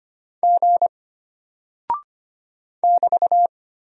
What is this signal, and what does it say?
Morse code, heard as a keyed tone, about 25 wpm, 700 Hz. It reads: G <BT>
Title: Single Character (Letter:Number:Punctuation:Prosign) Alt 25wpm